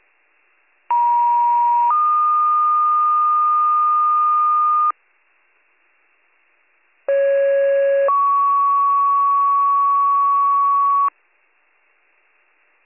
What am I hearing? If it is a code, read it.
Two-tone page, heard as a one-second tone, then a three-second tone, about 950 then 1220 Hz, about 550 then 1120 Hz. It is heard from 0.9 to 11.1 s.